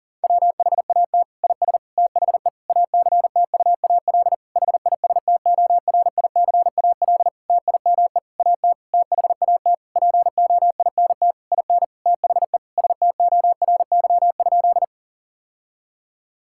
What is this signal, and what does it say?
Morse code, heard as a keyed tone, about 40 wpm, 700 Hz. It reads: WHAT IS THE ACTUAL HISTORICAL TIME AT THAT POINT IN THE STORY?
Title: Sentences from Top 1000 Words - Review Part 5 - 40wpm